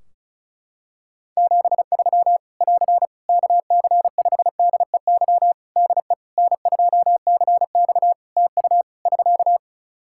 Morse code, heard as a keyed tone, about 36 wpm, 700 Hz. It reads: 73 <AR> KC5DEY DE N2CX TU <SK>
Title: Practice QSO - Exchange 4 - Ending - 35wpm